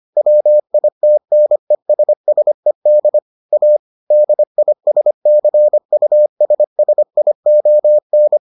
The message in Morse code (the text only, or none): WITNESSED A DISCUSSION